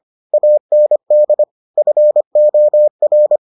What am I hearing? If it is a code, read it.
Morse code, heard as a keyed tone, about 25 wpm, 600 Hz. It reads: AND FOR